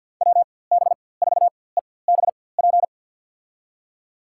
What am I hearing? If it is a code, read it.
Morse code, heard as a keyed tone, about 50 wpm, spaced as for 23 wpm, 700 Hz. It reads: WB4EBP
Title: USA Call Signs - 4x Character Spacing - 50wpm